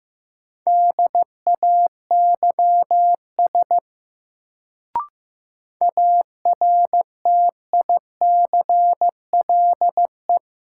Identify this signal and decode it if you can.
Morse code, heard as a keyed tone, about 16 wpm, 700 Hz. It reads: DAYS ARTICLE
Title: Sentences from Top 1000 Words - Review Part 5 - New Word Drill - 15wpm